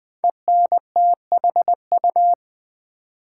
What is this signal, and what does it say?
Morse code, heard as a keyed tone, about 20 wpm, 700 Hz. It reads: ENTHU
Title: Build A Word - 5 Letters At A Time - 10-Letter Words - 20wpm